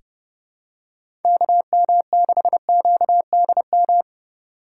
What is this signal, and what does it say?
Morse code, heard as a keyed tone, about 32 wpm, 700 Hz. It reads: KM6QDM
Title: USA Call Signs - 6 Characters - 3x Character Spacing - 30wpm